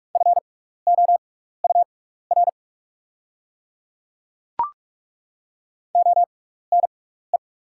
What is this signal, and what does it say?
Morse code, heard as a keyed tone, about 42 wpm, spaced as for 15 wpm, 690 Hz. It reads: FOUR ONE
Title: 6x Character Spacing - Top 300 Words - 45wpm